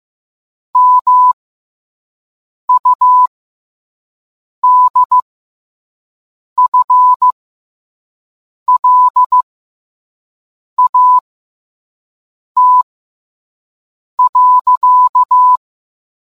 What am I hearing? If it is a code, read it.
Morse code, heard as a keyed tone, about 15 wpm, spaced as for 5 wpm, 1000 Hz. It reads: MUDFLAT.